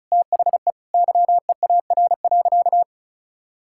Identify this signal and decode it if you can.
Morse code, heard as a keyed tone, about 36 wpm, 700 Hz. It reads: THE YEAR.
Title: Sentences from Top 1000 Words - Review Part 5 - 35wpm